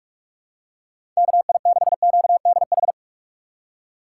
Morse code, heard as a keyed tone, about 43 wpm, 700 Hz. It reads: KI6QDH
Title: K1USN SST QSO Practice - 45wpm